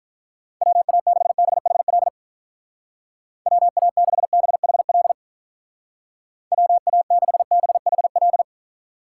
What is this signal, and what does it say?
Morse code, heard as a keyed tone, about 46 wpm, 700 Hz. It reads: WA6BHL WA6BHL WA6BHL